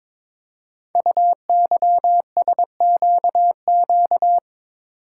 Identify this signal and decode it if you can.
Morse code, heard as a keyed tone, about 22 wpm, 700 Hz. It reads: UYSQQ
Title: Code Groups - 5 Letters - 22wpm